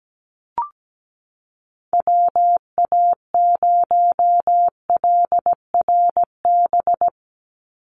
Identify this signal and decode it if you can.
Morse code, heard as a keyed tone, about 17 wpm, 700 Hz. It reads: WA0LRB
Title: USA Call Signs 17wpm